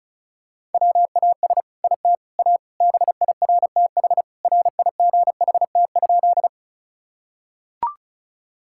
Morse code, heard as a keyed tone, about 36 wpm, 700 Hz. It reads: WAS IT A BIRTH RIGHT?
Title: Sentences from Top 1000 Words - Review Part 5 - 35wpm